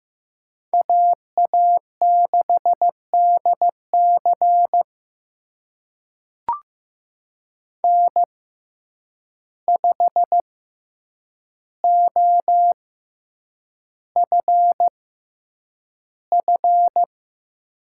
Morse code, heard as a keyed tone, about 15 wpm, spaced as for 8 wpm, 700 Hz. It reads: AA6DC N5OFF
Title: USA Call Signs - 6x Character Spacing - 15wpm